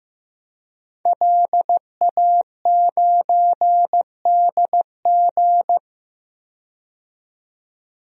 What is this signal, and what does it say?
Morse code, heard as a keyed tone, about 15 wpm, 700 Hz. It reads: LA9DG
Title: DXCC-100 Call Signs - 5 Characters - 15wpm